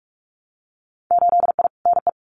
Morse code, heard as a keyed tone, about 44 wpm, 700 Hz. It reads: 8I NE
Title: Winter Field Day Exchanges Encore 45wpm